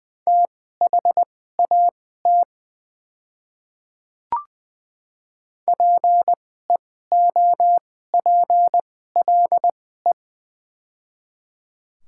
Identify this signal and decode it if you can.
Morse code, heard as a keyed tone, about 20 wpm, spaced as for 14 wpm, 700 Hz. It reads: THAT PEOPLE
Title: 2x Character Spacing - Top 200 Words - 20wpm